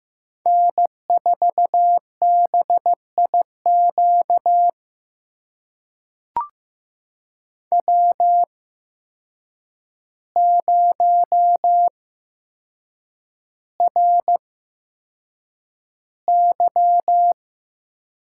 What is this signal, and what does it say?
Morse code, heard as a keyed tone, about 15 wpm, spaced as for 7 wpm, 700 Hz. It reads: N4BIQ W0RY
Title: USA Call Signs - 8x Character Spacing - 15wpm